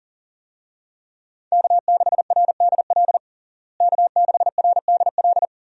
Morse code, heard as a keyed tone, about 40 wpm, 700 Hz. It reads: K6RDL K6RDL
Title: USA Call Signs - 5 Characters - Repeated 3x - 40wpm